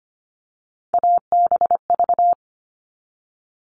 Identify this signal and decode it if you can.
Morse code, heard as a keyed tone, about 25 wpm, 700 Hz. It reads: ABV
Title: All Cut Numbers - 3-Digit 25wpm